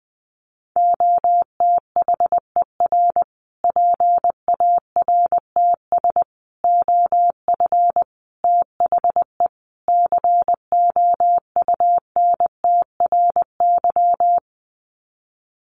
Morse code, heard as a keyed tone, about 20 wpm, 700 Hz. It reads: OTHER PARTS OF THE COUNTRY
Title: Sets of 5 Words Encore 20wpm